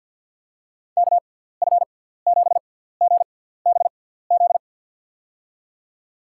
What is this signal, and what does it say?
Morse code, heard as a keyed tone, about 53 wpm, spaced as for 17 wpm, 700 Hz. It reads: KF7GBZ
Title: USA Call Signs - 6x Character Spacing - 50wpm